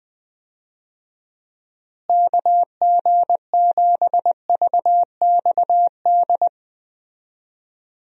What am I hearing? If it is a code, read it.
Morse code, heard as a keyed tone, about 20 wpm, 700 Hz. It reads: KG7VXD
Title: USA Call Signs - 6 Characters - Repeated 3x - 20wpm